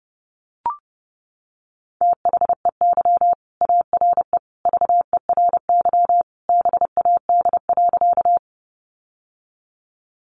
Morse code, heard as a keyed tone, about 30 wpm, 700 Hz. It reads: THEY ARE VERY BAD.
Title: Sentences from Top 700 Words 30wpm